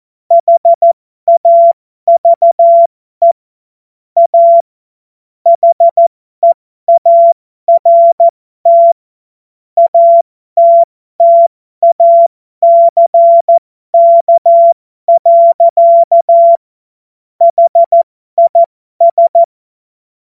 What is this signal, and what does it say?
Morse code, heard as a keyed tone, about 14 wpm, spaced as for 12 wpm, 680 Hz. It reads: HAVE A HEART ATTACK. HIS
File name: Morsekurs / Witze